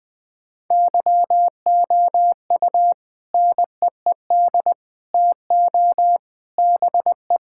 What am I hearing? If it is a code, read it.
Morse code, heard as a keyed tone, about 20 wpm, 700 Hz. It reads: YOU NEED TO BE
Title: Sentences with 3-Word Phrases - 20wpm